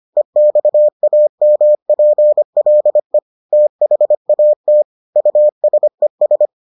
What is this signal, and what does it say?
Morse code, heard as a keyed tone, about 25 wpm, 600 Hz. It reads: EXAMPLE THAT USES